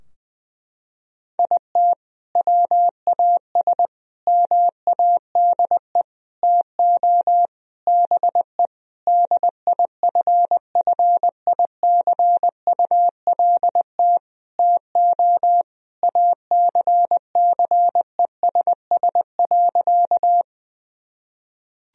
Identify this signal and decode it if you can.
Morse code, heard as a keyed tone, about 21 wpm, 700 Hz. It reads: IT WAS MADE TO BE DIFFICULT TO ACCESS.
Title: Sentences from Top 1000 Words - Review Part 4 - 20wpm